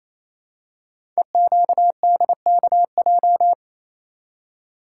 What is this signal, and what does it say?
Morse code, heard as a keyed tone, about 29 wpm, 700 Hz. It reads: EQDKJ
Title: Code Groups - 5 Letters - 28wpm